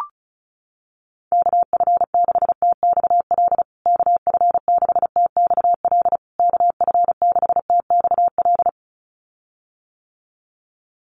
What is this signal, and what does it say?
Morse code, heard as a keyed tone, about 34 wpm, 700 Hz. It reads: KF6TXL KF6TXL KF6TXL
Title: USA Call Signs Repeated 3x 35wpm